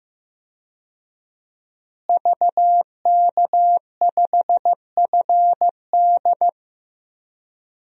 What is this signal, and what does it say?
Morse code, heard as a keyed tone, about 15 wpm, 700 Hz. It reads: VK5FD
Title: DXCC-100 Call Signs - 5 Characters - Repeated 3x - 15wpm